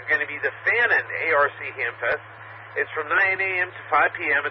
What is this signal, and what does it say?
Emergency Alert System (SAME) tones, heard as harsh buzzing bursts, with speech. It is heard from 0.3 to 3.7 s.